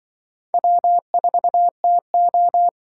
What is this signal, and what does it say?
Morse code, heard as a keyed tone, about 25 wpm, 700 Hz. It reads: W4TO